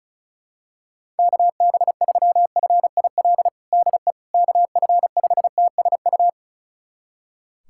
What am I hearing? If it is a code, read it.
Morse code, heard as a keyed tone, about 34 wpm, 700 Hz. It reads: KB3FIL DE KF5TSU
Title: USA Callsign DE Callsign - 6 Characters - 35wpm